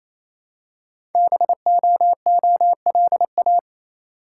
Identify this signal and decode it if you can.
Morse code, heard as a keyed tone, about 29 wpm, 700 Hz. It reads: BOOLA
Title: Code Groups - 5 Letters - 28wpm